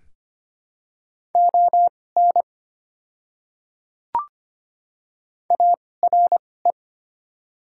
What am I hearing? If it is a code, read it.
Morse code, heard as a keyed tone, about 25 wpm, spaced as for 18 wpm, 700 Hz. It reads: ON ARE
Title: 2x Character Spacing - Top 100 Words - 25wpm